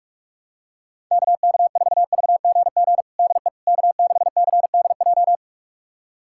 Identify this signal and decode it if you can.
Morse code, heard as a keyed tone, about 42 wpm, 700 Hz. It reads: KK4VGG DE K6CDJ
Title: USA Callsign DE Callsign 45wpm